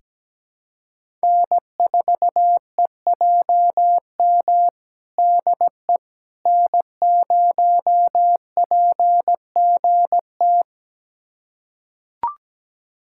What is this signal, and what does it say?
Morse code, heard as a keyed tone, about 17 wpm, 700 Hz. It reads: N4EJM DE N0PGT
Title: USA Callsign DE Callsign - 5 Characters - 17wpm